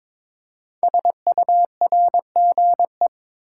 Morse code, heard as a keyed tone, about 22 wpm, 700 Hz. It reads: SURGE
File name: 5 Letter Words - First 3 Letters Given - 22wpm